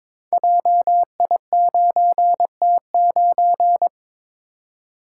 Morse code, heard as a keyed tone, about 23 wpm, 700 Hz. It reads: JI9T9
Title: Code Groups - 5 Letter:Numbers - 22wpm